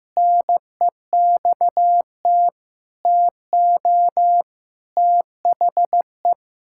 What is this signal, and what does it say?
Morse code, heard as a keyed tone, about 15 wpm, 700 Hz. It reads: NEXT TO THE
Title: Pangram Challenge v2 - 15wpm